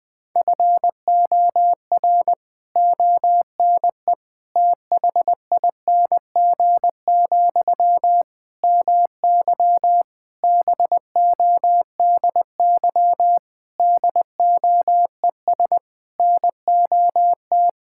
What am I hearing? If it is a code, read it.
Morse code, heard as a keyed tone, about 20 wpm, 700 Hz. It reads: FOR ONE THING, MY BODY DOES NOT
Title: Sentences from Top 1000 Words - Review Part 6 - 20wpm